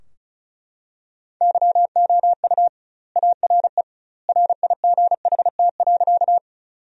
Morse code, heard as a keyed tone, about 34 wpm, 700 Hz. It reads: YOU ARE RIGHT.
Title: 2x Word-Spacing - Sentences from Top 200 Words - 35wpm